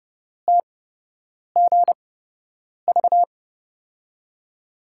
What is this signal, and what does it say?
Morse code, heard as a keyed tone, about 30 wpm, spaced as for 8 wpm, 700 Hz. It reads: TGV
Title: USA Call Signs - 5 Characters - 8x Character Spacing - 30wpm